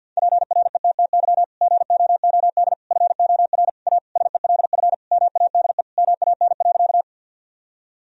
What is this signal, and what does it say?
Morse code, heard as a keyed tone, about 51 wpm, 700 Hz. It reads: PRETTY GOOD FOR A SELF MADE MAN.
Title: Sentences from Top 1000 Words - Review Part 1 - 50wpm